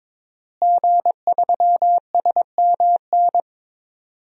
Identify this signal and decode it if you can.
Morse code, heard as a keyed tone, about 22 wpm, 700 Hz. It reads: G3SMN
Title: DXCC-100 Call Signs - 5 Characters - Repeated 3x - 22wpm